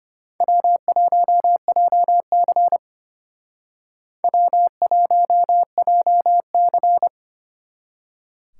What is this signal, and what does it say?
Morse code, heard as a keyed tone, about 32 wpm, 700 Hz. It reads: W1JC W1JC